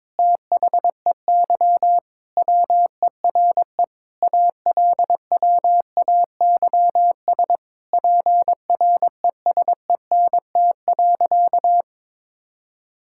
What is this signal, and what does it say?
Morse code, heard as a keyed tone, about 22 wpm, 700 Hz. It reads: THEY WERE ALWAYS PRESENT.
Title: Sentences from Top 1000 Words - Review Part 5 - 22wpm